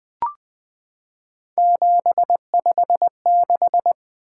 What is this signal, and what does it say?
Morse code, heard as a keyed tone, about 20 wpm, 700 Hz. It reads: 756
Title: Three Letter-Number Combinations - 20wpm